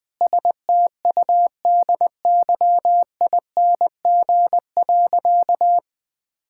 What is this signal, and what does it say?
Morse code, heard as a keyed tone, about 20 wpm, 700 Hz. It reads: STUDYING.
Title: Sentences with 6-Word Phrases - Spoken Beginning - 20wpm